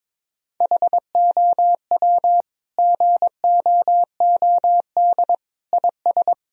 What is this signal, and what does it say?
Morse code, heard as a keyed tone, about 22 wpm, 700 Hz. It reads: HOW GOOD IS
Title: Sentences from Top 1000 Words - Review Part 4 - 22wpm